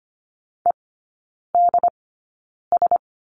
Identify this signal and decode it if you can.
Morse code, heard as a keyed tone, about 25 wpm, spaced as for 9 wpm, 700 Hz. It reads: EDS
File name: course / 25:10wpm / 093 Lesson 16e - Letter c - Three Characters 25wpm